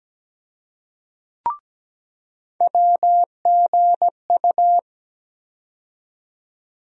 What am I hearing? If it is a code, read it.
Morse code, heard as a keyed tone, about 17 wpm, 700 Hz. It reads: WGU
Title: USA Call Sign Postfixes - 3 Letters 17wpm